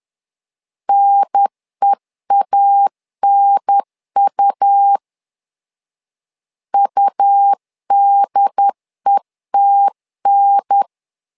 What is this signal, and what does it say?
Morse code, heard as a keyed tone, about 10 wpm, 800 Hz. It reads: NEANU UDETN
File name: LEZIONI / 2 UHDN / 2 UHDN 10wpm 1904